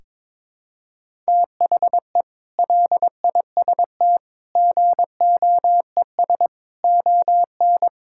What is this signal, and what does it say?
Morse code, heard as a keyed tone, about 22 wpm, 700 Hz. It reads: THE LIST GOES ON
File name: practice / 22wpm / Sentences with 8-Word Phrases - 22wpm